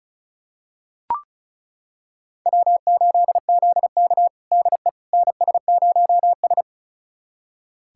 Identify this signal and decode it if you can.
Morse code, heard as a keyed tone, about 36 wpm, 700 Hz. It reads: W8ZK DE NS0S
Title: USA Callsign DE Callsign - 4 Characters - 35wpm